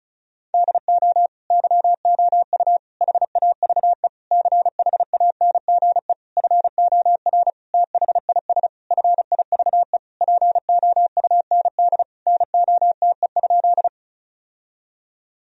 Morse code, heard as a keyed tone, about 36 wpm, 700 Hz. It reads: DO YOU HAVE CHANGE FOR THIS FIVE POUND NOTE?
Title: Sentences from Top 1000 Words - Review Part 2 - 35wpm